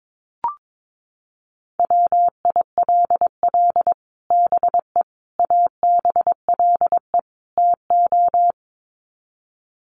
Morse code, heard as a keyed tone, about 22 wpm, 700 Hz. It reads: WILL BE ABLE TO
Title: Sets of 4 Words 22wpm